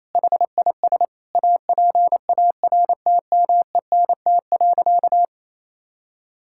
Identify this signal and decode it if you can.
Morse code, heard as a keyed tone, about 28 wpm, 700 Hz. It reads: HIS APARTMENT.
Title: Sentences from Top 1000 Words - Review Part 6 - 28wpm